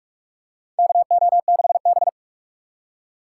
Morse code, heard as a keyed tone, about 46 wpm, 700 Hz. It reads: KO6B
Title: USA Call Signs - 2x Character Spacing - 45wpm